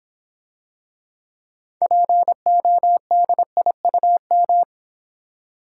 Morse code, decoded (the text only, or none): PODIUM